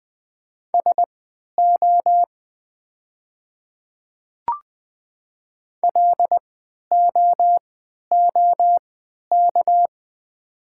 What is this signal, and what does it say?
Morse code, heard as a keyed tone, about 21 wpm, spaced as for 12 wpm, 700 Hz. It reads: SO LOOK
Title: 3x Character Spacing - Top 100 Words - 20wpm